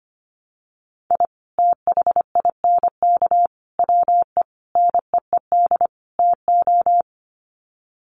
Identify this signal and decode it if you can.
Morse code, heard as a keyed tone, about 25 wpm, 700 Hz. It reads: I THINK WE NEED TO
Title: Sets of 5 Words 25wpm